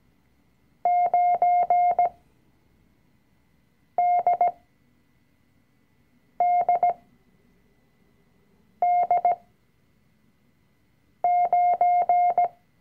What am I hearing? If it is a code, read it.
Morse code, heard as a keyed tone, about 17 wpm, spaced as for 4 wpm, 710 Hz. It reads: 9DDD9
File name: CW-Online-Kurs / Anfänger / Lektion 005